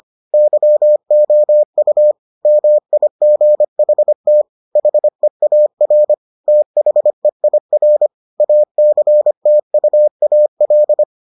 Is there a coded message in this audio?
Yes: Morse code from 0.3 to 11.0 s, heard as a keyed tone, about 25 wpm, 600 Hz. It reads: YOU MIGHT HEAR THEIR ACTUAL